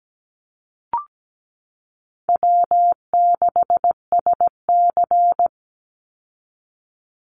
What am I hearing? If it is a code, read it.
Morse code, heard as a keyed tone, about 17 wpm, 700 Hz. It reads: W6SC
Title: Call Signs Repeated 3x - 17wpm